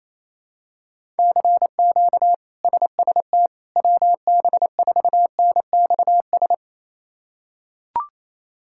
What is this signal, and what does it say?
Morse code, heard as a keyed tone, about 28 wpm, 700 Hz. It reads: CQ SST WB4NXS
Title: K1USN SST QSO Practice - 28wpm